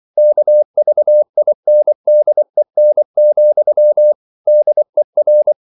Morse code, heard as a keyed tone, about 24 wpm, 600 Hz. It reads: KVINDEN, DER
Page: Ferie 24 wpm